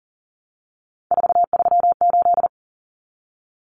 Morse code, heard as a keyed tone, about 40 wpm, 700 Hz. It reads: 438
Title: Common Cut Numbers - 3-Digit 40wpm